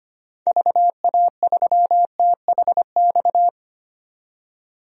Morse code, heard as a keyed tone, about 25 wpm, 690 Hz. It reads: VA3THX
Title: DXCC-100 Call Signs - 6 Characters - 25wpm